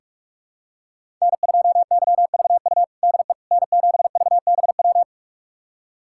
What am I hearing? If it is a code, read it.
Morse code, heard as a keyed tone, about 46 wpm, 700 Hz. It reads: N2YVU DE N7VBW
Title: USA Callsign DE Callsign 45wpm